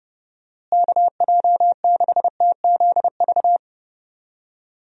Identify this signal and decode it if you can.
Morse code, heard as a keyed tone, about 31 wpm, 700 Hz. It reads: KJ6TZV
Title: USA Call Signs - 6 Characters - Repeated 2x - 30wpm